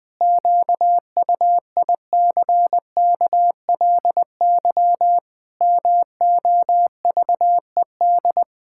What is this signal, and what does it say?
Morse code, heard as a keyed tone, about 20 wpm, 700 Hz. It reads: QUICKLY MOVED
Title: Pangram Challenge v2 - 20wpm